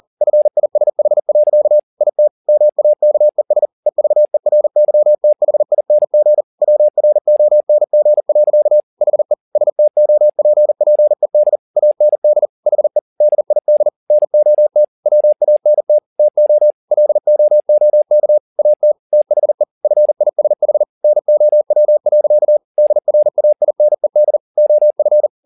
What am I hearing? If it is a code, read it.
Morse code, heard as a keyed tone, about 40 wpm, 600 Hz. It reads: FISH. IT MAKES EVERYTHING WRONG. HE STOPPED AND HE DID NOT WANT TO LOOK AT THE FISH NOW. DRAINED OF
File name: the old man and the sea 2 40wpm